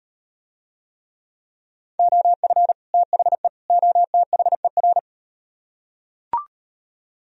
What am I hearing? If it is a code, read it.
Morse code, heard as a keyed tone, about 38 wpm, 700 Hz. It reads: OF THE OTHER